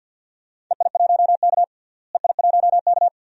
Morse code, heard as a keyed tone, about 47 wpm, 700 Hz. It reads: EI1K EI1K
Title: DXCC-100 Call Signs - 4 Characters - Repeated 2x - 50wpm